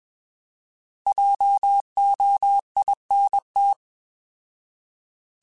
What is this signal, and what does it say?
Morse code, heard as a keyed tone, about 21 wpm, 800 Hz. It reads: JOINT
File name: Alphabetical / Alphabetical mp3s / Js @20